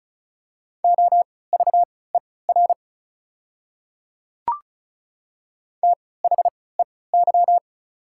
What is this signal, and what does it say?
Morse code, heard as a keyed tone, about 36 wpm, spaced as for 20 wpm, 690 Hz. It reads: OVER THEY
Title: 3x Character Spacing - Top 100 Words - 35wpm